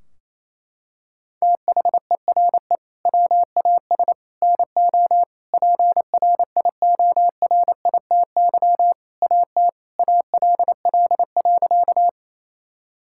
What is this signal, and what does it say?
Morse code, heard as a keyed tone, about 28 wpm, 700 Hz. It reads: THERE WAS NO PRIORITY AT ALL.
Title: Sentences from Top 1600 Words 28wpm